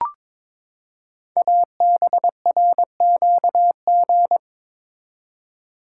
Morse code, heard as a keyed tone, about 22 wpm, 700 Hz. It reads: ABRQG
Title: Code Groups - 5 Letters - 22wpm